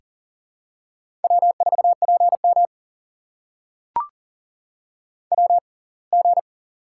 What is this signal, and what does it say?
Morse code, heard as a keyed tone, about 40 wpm, spaced as for 27 wpm, 700 Hz. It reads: W4PM WG
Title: USA Call Signs - 4 Characters - 6x Character Spacing - 40wpm